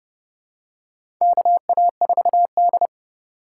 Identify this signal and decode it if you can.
Morse code, heard as a keyed tone, about 29 wpm, 700 Hz. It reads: KA4D
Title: USA Call Signs - 4 Characters - 6x Character Spacing - 30wpm